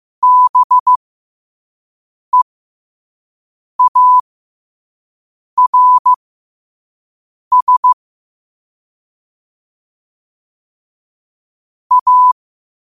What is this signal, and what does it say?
Morse code, heard as a keyed tone, about 15 wpm, spaced as for 5 wpm, 1000 Hz. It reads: BEARS A
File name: qotdfiles / 5wpm